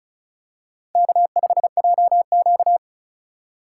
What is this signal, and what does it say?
Morse code, heard as a keyed tone, about 36 wpm, 700 Hz. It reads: K5JQ
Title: USA Call Signs - 4x Character Spacing - 35wpm